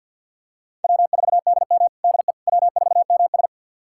Morse code, heard as a keyed tone, about 51 wpm, 700 Hz. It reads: W4DM DE W4MS
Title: USA Callsign DE Callsign - 4 Characters - 50wpm